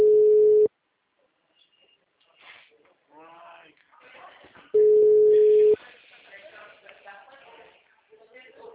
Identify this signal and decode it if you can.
Ringback tone (Europe, 425 Hz), with speech.
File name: out- -2011- - - .86437